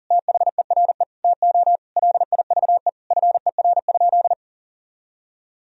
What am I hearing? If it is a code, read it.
Morse code, heard as a keyed tone, about 40 wpm, 700 Hz. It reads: THERE TO LIVE FER?
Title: Sentences from Top 200 Words and Abbreviations 40wpm